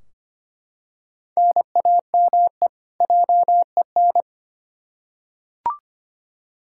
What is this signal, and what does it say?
Morse code, heard as a keyed tone, about 25 wpm, 700 Hz. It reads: NAME JEN
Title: K1USN SST QSO Practice 25wpm